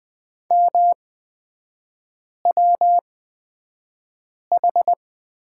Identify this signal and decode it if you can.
Morse code, heard as a keyed tone, about 20 wpm, spaced as for 5 wpm, 700 Hz. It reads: MWH
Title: 250 Lesson 40e - BK - Three Characters 20wpm